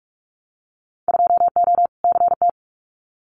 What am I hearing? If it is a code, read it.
Morse code, heard as a keyed tone, about 46 wpm, 700 Hz. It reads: 2O CT
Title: Winter Field Day Exchanges 45wpm